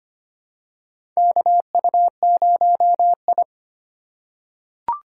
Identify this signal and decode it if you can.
Morse code, heard as a keyed tone, about 25 wpm, 700 Hz. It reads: KU0I
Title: USA Call Signs - 4 Characters - 4x Character Spacing - 25wpm